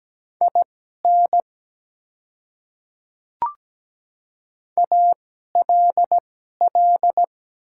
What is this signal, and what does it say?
Morse code, heard as a keyed tone, about 17 wpm, spaced as for 12 wpm, 700 Hz. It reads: IN ALL
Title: 2x Character Spacing - Top 100 Words - 17wpm